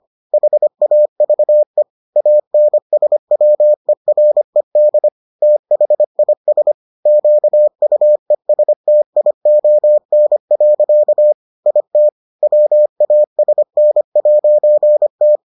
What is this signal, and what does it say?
Morse code, heard as a keyed tone, about 25 wpm, 600 Hz. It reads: HAVE ANSWERED THIS QUESTION. IT WASN'T